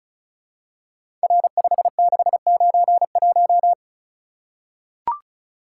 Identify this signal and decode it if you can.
Morse code, heard as a keyed tone, about 34 wpm, 700 Hz. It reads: R5691
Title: Code Groups - 5 Letter:Numbers - 35wpm